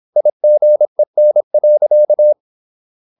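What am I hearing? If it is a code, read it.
Morse code, heard as a keyed tone, about 26 wpm, 600 Hz. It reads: IGEN.
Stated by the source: High-speed (QRQ) Morse at 26 wpm.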